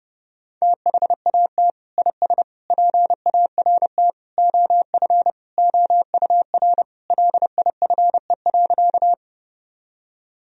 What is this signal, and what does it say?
Morse code, heard as a keyed tone, about 32 wpm, 700 Hz. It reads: THAT IS PART OF OUR LIFE.
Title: Sentences from Top 1000 Words - Review Part 1 - 30wpm